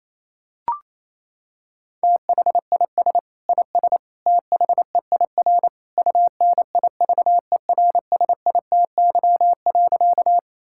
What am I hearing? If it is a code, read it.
Morse code, heard as a keyed tone, about 28 wpm, 700 Hz. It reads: THIS IS THEIR UNIVERSITY.
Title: Sentences from Top 1000 Words - Review Part 5 - 28wpm